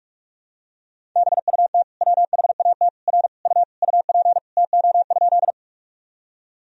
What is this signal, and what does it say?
Morse code, heard as a keyed tone, about 46 wpm, 700 Hz. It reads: BUT WHAT R U UP TO?
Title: Sentences from Top 100 Words and Abbreviations 45wpm